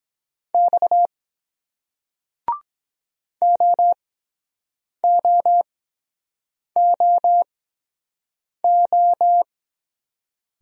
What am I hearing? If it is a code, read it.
Morse code, heard as a keyed tone, about 22 wpm, spaced as for 7 wpm, 700 Hz. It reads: X OOOO